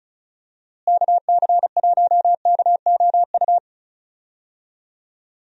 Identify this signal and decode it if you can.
Morse code, heard as a keyed tone, about 34 wpm, 700 Hz. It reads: KC1KOU
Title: USA Call Signs - 6 Characters - Repeated 3x - 35wpm